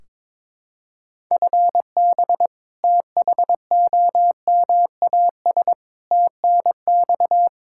Morse code, heard as a keyed tone, about 22 wpm, 700 Hz. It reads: FB THOMAS TNX
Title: Practice QSO - Exchange 3 - More Info - 22wpm